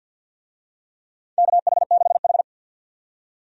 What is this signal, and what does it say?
Morse code, heard as a keyed tone, about 52 wpm, 700 Hz. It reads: KH6H
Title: USA Call Signs - 4 Characters - 50wpm